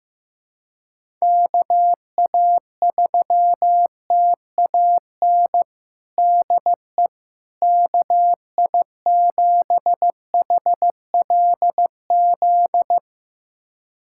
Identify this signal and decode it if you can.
Morse code, heard as a keyed tone, about 15 wpm, 700 Hz. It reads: KA3TAN DE KI7HLZ
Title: USA Callsign DE Callsign - 6 Characters - 15wpm